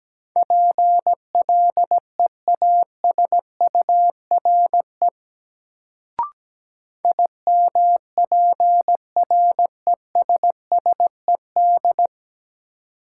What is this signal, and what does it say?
Morse code, heard as a keyed tone, about 17 wpm, 700 Hz. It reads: PLEASURE IMPRESSED
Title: Sentences from Top 1600 Words - New Word Drill Alt - 17wpm